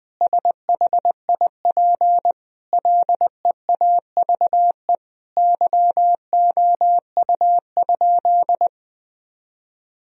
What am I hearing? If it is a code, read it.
Morse code, heard as a keyed tone, about 20 wpm, 700 Hz. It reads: SHIP LEAVE YOU?
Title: Sentences from Top 1000 Words - Review Part 2 - 20wpm